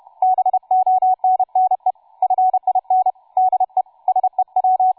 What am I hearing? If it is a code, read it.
Morse code, heard as a keyed tone, about 31 wpm, 750 Hz, with radio noise. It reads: BONNE FIN DE SEW
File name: cw forum / tu2 f5ldy 18mhz 16h46